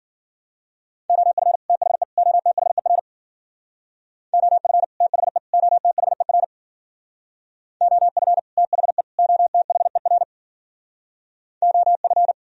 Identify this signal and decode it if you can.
Morse code, heard as a keyed tone, about 49 wpm, 700 Hz. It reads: OF THE OTHER OF THE OTHER OF THE OTHER OF
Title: Speed-Racing - Sets of 3 Words 40wpm